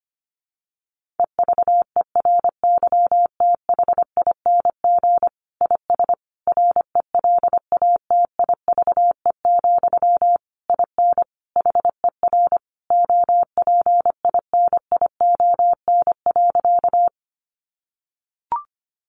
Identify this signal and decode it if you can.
Morse code, heard as a keyed tone, about 25 wpm, 700 Hz. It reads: EVERYTHING IS RELATIVE, IN HER OPINION.
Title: Sentences from Top 1400 Words 25wpm